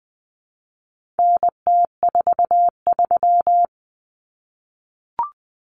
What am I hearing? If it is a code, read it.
Morse code, heard as a keyed tone, about 20 wpm, 700 Hz. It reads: NT43
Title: Common Cut Numbers - 4-Digit 20wpm